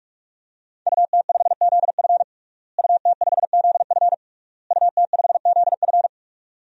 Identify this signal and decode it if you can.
Morse code, heard as a keyed tone, about 46 wpm, 700 Hz. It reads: UT5ZF UT5ZF UT5ZF
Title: DXCC-100 Call Signs - 5 Characters - Repeated 3x - 45wpm